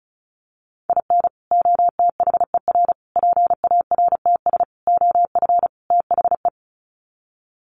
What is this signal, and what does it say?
Morse code, heard as a keyed tone, about 35 wpm, 700 Hz. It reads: IN OTHER PARTS OF THE
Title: Sets of 5 Words 35wpm